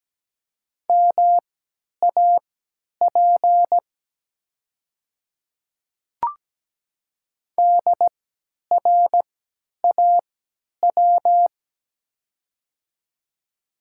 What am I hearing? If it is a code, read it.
Morse code, heard as a keyed tone, about 17 wpm, spaced as for 10 wpm, 700 Hz. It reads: MAP DRAW
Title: 3x Character Spacing - Top 500 Words - 17wpm